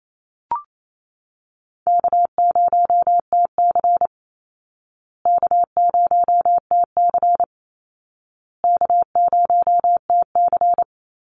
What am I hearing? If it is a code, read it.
Morse code, heard as a keyed tone, about 28 wpm, 700 Hz. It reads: K0TC K0TC K0TC
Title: Call Signs Repeated 2x - 28wpm